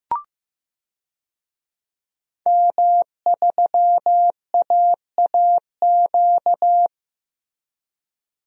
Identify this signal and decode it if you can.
Morse code, heard as a keyed tone, about 15 wpm, 700 Hz. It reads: M3AAQ
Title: DXCC-100 Call Signs - 5 Characters - Repeated 3x - 15wpm